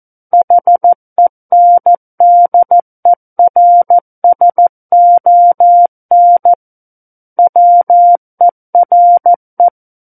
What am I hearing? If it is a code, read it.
Morse code, heard as a keyed tone, about 14 wpm, 700 Hz. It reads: HENDERSON WERE
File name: Book One / 05-Chapter 5 - 14 WPM